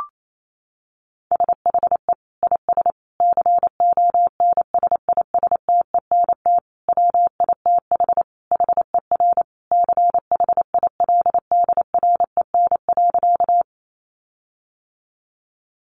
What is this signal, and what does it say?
Morse code, heard as a keyed tone, about 28 wpm, 700 Hz. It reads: SHE IS CONSISTENT WITH HER CHILDREN.
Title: Sentences from Top 1500 Words 28wpm